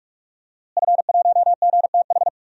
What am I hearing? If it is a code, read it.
Morse code, heard as a keyed tone, about 46 wpm, 700 Hz. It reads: F1GTH
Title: DXCC-100 Call Signs - 5 Characters - Repeated 3x - 45wpm